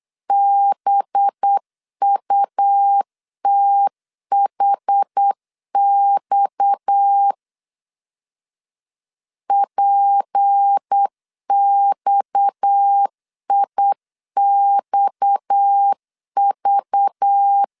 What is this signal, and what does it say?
Morse code, heard as a keyed tone, about 8 wpm, 800 Hz. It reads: BUTHX PXIXV
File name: LEZIONI / 6 PKX0 / 6 PKX 8wpm 3041